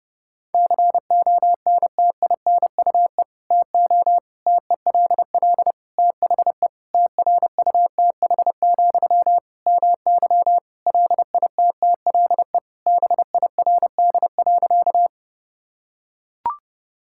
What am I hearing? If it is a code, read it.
Morse code, heard as a keyed tone, about 32 wpm, 700 Hz. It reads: CONTINUE TO TELL THE TRUTH, MY LITTLE BIRD.
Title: Sentences from Top 1000 Words - Review Part 6 - 30wpm